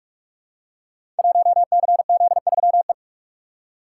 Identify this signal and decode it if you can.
Morse code, heard as a keyed tone, about 43 wpm, 700 Hz. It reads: 1CZ3E
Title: Code Groups - 5 Letter:Numbers - 45wpm